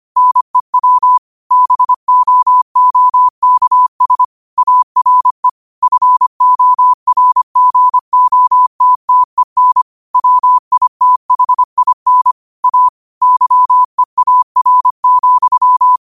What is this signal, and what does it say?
Morse code, heard as a keyed tone, about 25 wpm, 1000 Hz. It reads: NEW BOOKS ARE FORGOTTEN WITHIN A YEAR,